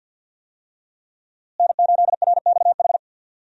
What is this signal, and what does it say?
Morse code, heard as a keyed tone, about 52 wpm, 700 Hz. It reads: N8RXH
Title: USA Call Signs - 5 Characters - Repeated 3x - 50wpm